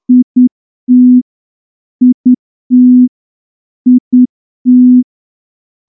Morse code, heard as a keyed tone, about 9 wpm, 250 Hz. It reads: IT IT IT
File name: morse-training-10WPM